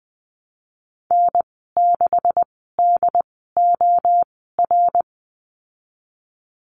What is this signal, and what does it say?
Morse code, heard as a keyed tone, about 20 wpm, spaced as for 15 wpm, 700 Hz. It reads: N6DOR
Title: 140 Lesson 23g - Letter f - Call Signs 20wpm